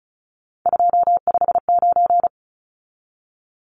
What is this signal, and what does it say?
Morse code, heard as a keyed tone, about 34 wpm, 700 Hz. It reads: 259
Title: Numbers - 3-Digit 35wpm